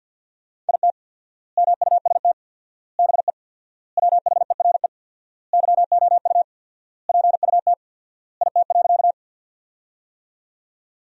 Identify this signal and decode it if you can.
Morse code, heard as a keyed tone, about 48 wpm, 700 Hz. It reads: IT MUST BE WHERE YOU PUT IT.
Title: 4x Word-Spacing - Sentences from Top 200 Words - 50wpm